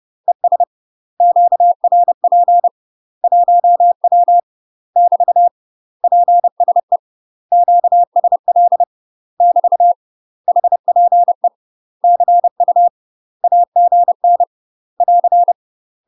Morse code, heard as a keyed tone, about 30 wpm, 700 Hz. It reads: ES QRP 1W <BT> PSE QSL <BT> HPE CU AGN <AR>
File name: UK MASTER QSOs TRAINING / 30WPM / UK MASTER QSOs 0015